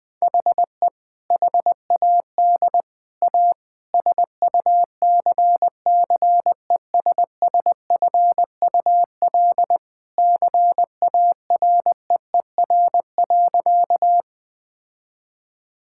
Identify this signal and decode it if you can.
Morse code, heard as a keyed tone, about 20 wpm, 700 Hz. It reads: HE HAD A SUCCESSFUL CAREER.
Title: Sentences from Top 1000 Words - Review Part 6 - 20wpm